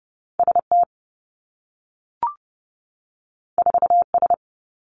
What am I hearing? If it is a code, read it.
Morse code, heard as a keyed tone, about 30 wpm, 700 Hz. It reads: ST 4S
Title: Two Challenging Letter-Number Combinations 30wpm